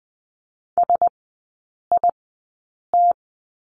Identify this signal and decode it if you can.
Morse code, heard as a keyed tone, about 19 wpm, spaced as for 8 wpm, 700 Hz. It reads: SIT
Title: 127 Lesson 21e - Number 6 - Three Characters 20wpm